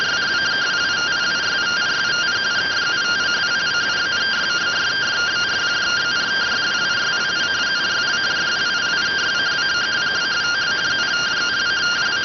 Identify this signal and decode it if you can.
Two-tone teleprinter signal, about 170 Hz shift.